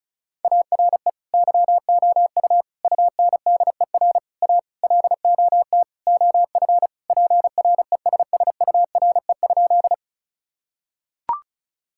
Morse code, heard as a keyed tone, about 36 wpm, 700 Hz. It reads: ARE YOU UNDER A LOT OF PRESSURE?
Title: Sentences from Top 1000 Words - Review Part 6 - 35wpm